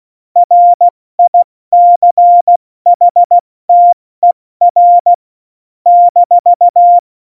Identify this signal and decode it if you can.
Morse code, heard as a keyed tone, about 16 wpm, 700 Hz. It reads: RICHTER -
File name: Morsekurs / Witze / Witz 05. Januar 2021, 70 F80